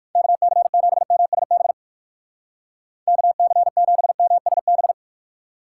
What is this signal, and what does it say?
Morse code, heard as a keyed tone, about 57 wpm, 700 Hz. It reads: KC7MSB KC7MSB